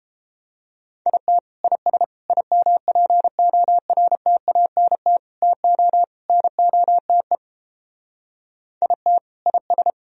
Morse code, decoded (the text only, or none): IT IS IMPORTANT TO NOTE IT IS